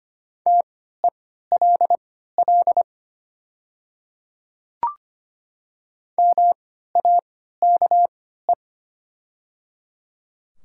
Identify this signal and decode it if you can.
Morse code, heard as a keyed tone, about 25 wpm, spaced as for 14 wpm, 700 Hz. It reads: TELL MAKE
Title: 3x Character Spacing - Top 200 Words - 25wpm